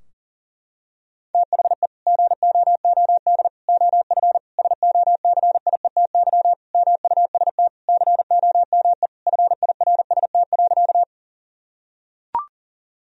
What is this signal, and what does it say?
Morse code, heard as a keyed tone, about 40 wpm, 700 Hz. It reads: THE GOOD OF SOCIETY MUST COME FIRST.
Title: Sentences from Top 1000 Words - Review Part 4 - 40wpm